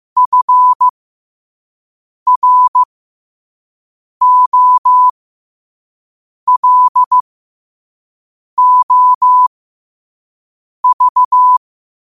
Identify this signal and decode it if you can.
Morse code, heard as a keyed tone, about 15 wpm, spaced as for 5 wpm, 1000 Hz. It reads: FROLOV
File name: qotdfiles / 5wpm